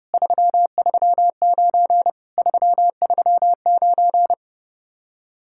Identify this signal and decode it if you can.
Morse code, heard as a keyed tone, about 36 wpm, 700 Hz. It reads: 339 339
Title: Speed-Racing Common QSO Phrases Encore - 30wpm